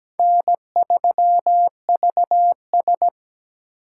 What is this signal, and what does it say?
Morse code, heard as a keyed tone, about 17 wpm, 700 Hz. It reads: N3VS
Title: USA Call Signs - 2x Character Spacing - 17wpm